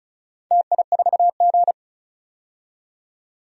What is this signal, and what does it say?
Morse code, heard as a keyed tone, about 36 wpm, 700 Hz. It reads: TI4G
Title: DXCC-100 Call Signs - 4 Characters - 35wpm